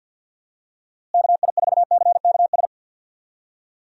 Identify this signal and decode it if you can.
Morse code, heard as a keyed tone, about 52 wpm, 700 Hz. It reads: KI4CKS
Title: USA Call Signs - 6 Characters - Repeated 3x - 50wpm